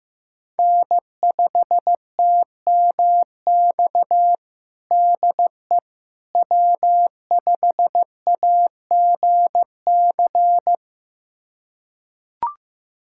Morse code, heard as a keyed tone, about 15 wpm, 700 Hz. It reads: N5TMX DE W5AGC
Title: USA Callsign DE Callsign - 5 Characters - 15wpm